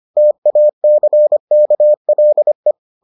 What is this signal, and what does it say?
Morse code, heard as a keyed tone, about 25 wpm, 600 Hz. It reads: TACKLE